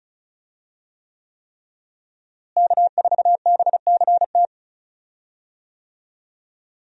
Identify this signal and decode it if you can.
Morse code, heard as a keyed tone, about 36 wpm, 700 Hz. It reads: K4BCT
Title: USA Call Signs - 5 Characters - Repeated 3x - 35wpm